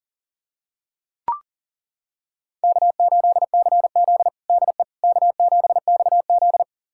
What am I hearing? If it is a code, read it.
Morse code, heard as a keyed tone, about 37 wpm, 700 Hz. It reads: K8CZ DE K7XZ
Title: USA Callsign DE Callsign - 4 Characters - 40wpm